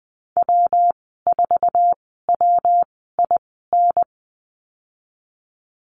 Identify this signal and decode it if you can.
Morse code, heard as a keyed tone, about 20 wpm, spaced as for 15 wpm, 700 Hz. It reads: W4WIN
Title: 173 Lesson 28g - Number 9 - Call Signs 20wpm